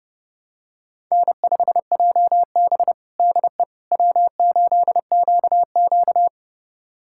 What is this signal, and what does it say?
Morse code, heard as a keyed tone, about 30 wpm, 700 Hz. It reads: N5JB DE W8QQ
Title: USA Callsign DE Callsign - 4 Characters - 30wpm